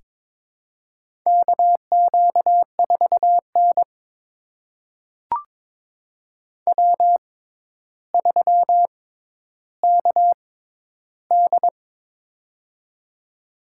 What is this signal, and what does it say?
Morse code, heard as a keyed tone, about 22 wpm, spaced as for 11 wpm, 700 Hz. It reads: KQ4N W3KD
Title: USA Call Signs - 4 Characters - 6x Character Spacing - 22wpm